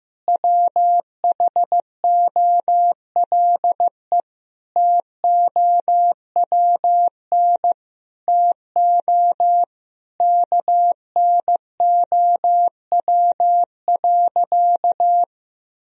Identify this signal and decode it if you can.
Morse code, heard as a keyed tone, about 15 wpm, 700 Hz. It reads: WHOLE TOWN TO KNOW.
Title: Sentences from Top 1000 Words - Review Part 2 - 15wpm